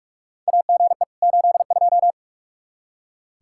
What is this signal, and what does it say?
Morse code, heard as a keyed tone, about 45 wpm, 700 Hz. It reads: AGE 82
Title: K1USN SST QSO Practice 45wpm